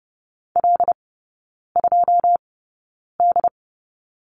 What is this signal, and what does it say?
Morse code, heard as a keyed tone, about 30 wpm, spaced as for 9 wpm, 700 Hz. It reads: L2D
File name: course / 30:10wpm / 087 Lesson 15e - Number 5 - Three Characters 30wpm